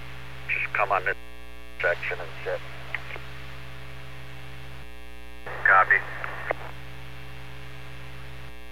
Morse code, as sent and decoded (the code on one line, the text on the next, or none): none